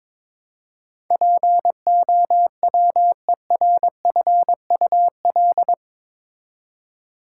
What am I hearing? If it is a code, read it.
Morse code, heard as a keyed tone, about 21 wpm, 700 Hz. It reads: POWERFUL
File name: practice / 22wpm / Eight Letter Words - 22wpm